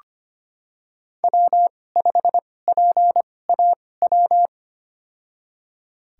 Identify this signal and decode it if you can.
Morse code, heard as a keyed tone, about 25 wpm, spaced as for 18 wpm, 700 Hz. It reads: W5PAW
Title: USA Call Signs - 5 Characters - 2x Character Spacing - 25wpm